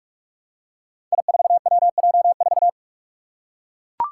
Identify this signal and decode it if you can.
Morse code, heard as a keyed tone, about 44 wpm, 700 Hz. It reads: I4WJ4
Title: Five Challenging Letter-Number Combinations 45wpm